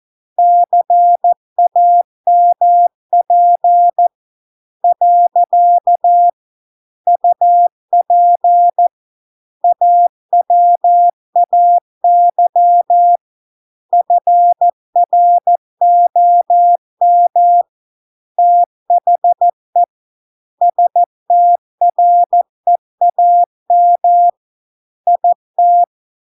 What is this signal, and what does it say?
Morse code, heard as a keyed tone, about 14 wpm, 700 Hz. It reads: CAMP . UP AWAY FROM THE STREAM IT